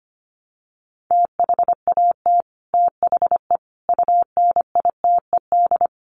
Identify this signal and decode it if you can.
Morse code, heard as a keyed tone, about 25 wpm, 700 Hz. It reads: THAT THE UNITED
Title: Sets of 5 Words Encore 25wpm